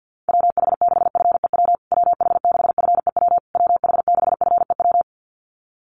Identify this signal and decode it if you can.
Morse code, heard as a keyed tone, about 50 wpm, 700 Hz. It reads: WH6FEW WH6FEW WH6FEW
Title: USA Call Signs Repeated 3x 50wpm